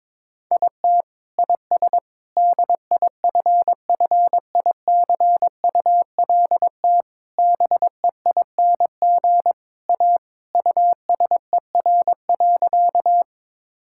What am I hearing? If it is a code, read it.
Morse code, heard as a keyed tone, about 22 wpm, 700 Hz. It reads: IT IS DIFFICULT BEING A USER.
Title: Sentences from Top 1000 Words - Review Part 5 - 22wpm